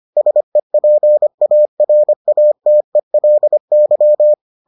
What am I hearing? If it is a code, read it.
Morse code, heard as a keyed tone, about 25 wpm, 600 Hz. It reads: SEPARATELY